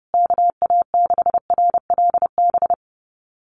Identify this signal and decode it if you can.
Morse code, heard as a keyed tone, about 30 wpm, 700 Hz. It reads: KA6RLB
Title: USA Call Signs Repeated 3x 30wpm